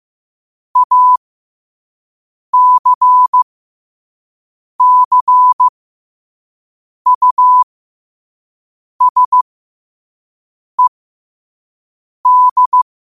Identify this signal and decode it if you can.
Morse code, heard as a keyed tone, about 15 wpm, spaced as for 5 wpm, 1000 Hz. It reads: ACCUSED
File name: qotdfiles / 5wpm